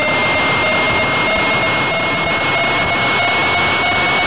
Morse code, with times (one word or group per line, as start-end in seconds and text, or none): none